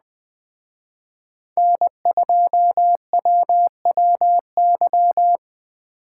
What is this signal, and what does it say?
Morse code, heard as a keyed tone, about 21 wpm, 700 Hz. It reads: N2WWY